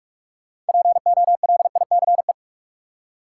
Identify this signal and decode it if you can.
Morse code, heard as a keyed tone, about 42 wpm, 700 Hz. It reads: POLICE
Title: Six Letter Words - 45wpm